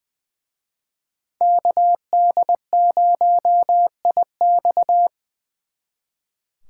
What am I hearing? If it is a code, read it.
Morse code, heard as a keyed tone, about 23 wpm, 700 Hz. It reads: KD0IX
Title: Speed Racing - USA 2x2 Call Signs 20wpm